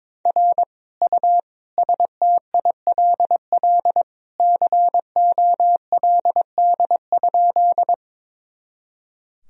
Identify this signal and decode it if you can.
Morse code, heard as a keyed tone, about 22 wpm, 700 Hz. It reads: R U STILL COLD?
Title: Sentences from Top 400 Words and Abbreviations 22wpm